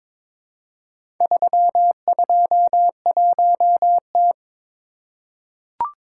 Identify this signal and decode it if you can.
Morse code, heard as a keyed tone, about 22 wpm, 690 Hz. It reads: 321T
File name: practice / 22wpm / Four Letter-Number Combinations - 22wpm